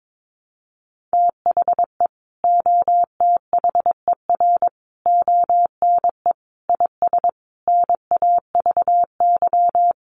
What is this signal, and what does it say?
Morse code, heard as a keyed tone, about 22 wpm, 700 Hz. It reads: THE OTHER ONE IS NAVY
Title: Additional Easy Phrases 22wpm